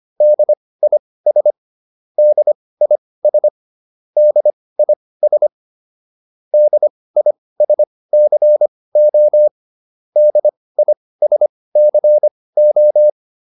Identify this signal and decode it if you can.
Morse code, heard as a keyed tone, about 25 wpm, spaced as for 18 wpm, 600 Hz. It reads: DIS DIS DIS DISCO DISCO